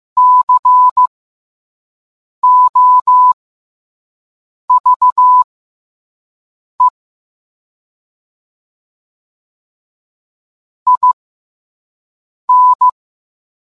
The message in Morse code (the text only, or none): COVE IN